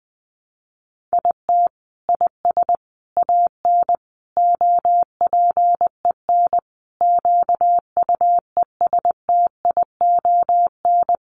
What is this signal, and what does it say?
Morse code, heard as a keyed tone, about 20 wpm, 700 Hz. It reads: IT IS AN OPEN QUESTION
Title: Sentences from Top 1300 Words 20wpm